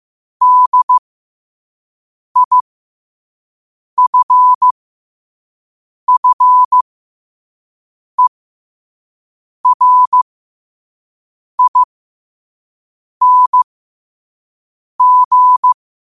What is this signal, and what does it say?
Morse code, heard as a keyed tone, about 15 wpm, spaced as for 5 wpm, 1000 Hz. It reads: DIFFERING
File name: qotdfiles / 5wpm